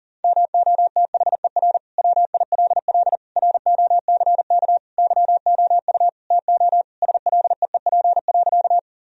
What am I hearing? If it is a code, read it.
Morse code, heard as a keyed tone, about 37 wpm, 700 Hz. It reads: MOTHER WILL ROCK YOU TO SLEEP.
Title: Sentences from Top 1000 Words - Review Part 2 - 40wpm